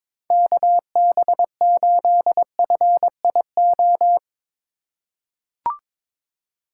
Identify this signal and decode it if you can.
Morse code, heard as a keyed tone, about 22 wpm, 700 Hz. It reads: KB8FIO
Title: USA Call Signs - 6 Characters - 3x Character Spacing - 22wpm